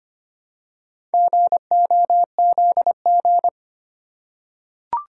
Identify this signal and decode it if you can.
Morse code, heard as a keyed tone, about 25 wpm, 700 Hz. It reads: GOZG
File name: practice / 25wpm / Four Letter-Number Combinations - 25wpm